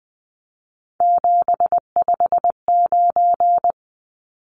Morse code, heard as a keyed tone, about 20 wpm, 700 Hz. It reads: 759
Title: Numbers - 3-Digit 20wpm